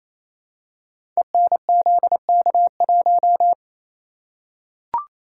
Morse code, heard as a keyed tone, about 29 wpm, 700 Hz. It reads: ENZK1
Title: Code Groups - 5 Letter:Numbers - 28wpm